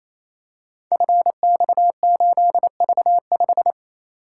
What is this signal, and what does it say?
Morse code, heard as a keyed tone, about 27 wpm, 700 Hz. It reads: FX8V5
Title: Code Groups - 5 Letter:Numbers - 28wpm